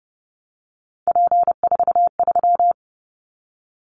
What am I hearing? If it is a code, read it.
Morse code, heard as a keyed tone, about 30 wpm, 700 Hz. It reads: P43
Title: Callsign Prefixes for Top 100 DXCC - 30wpm